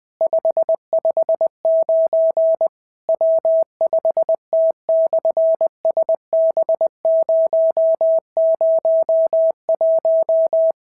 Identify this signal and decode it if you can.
Morse code, heard as a keyed tone, about 19 wpm, 650 Hz. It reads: 559 W5T/SB001
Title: SOTA QSO Practice 20wpm